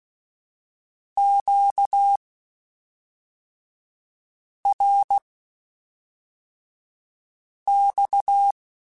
Morse code, heard as a keyed tone, about 16 wpm, spaced as for 3 wpm, 800 Hz. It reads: QRX
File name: (spaced) / Q-Calls Spaced-mp3s / Q Calls sp Once @5